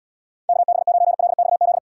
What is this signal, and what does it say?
Morse code, heard as a keyed tone, about 78 wpm, 700 Hz. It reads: KB9LXZ